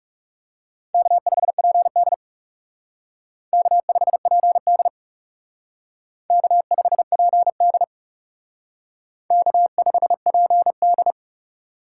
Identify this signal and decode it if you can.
Morse code, heard as a keyed tone, about 38 wpm, 700 Hz. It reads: K5PD K5PD K5PD K5PD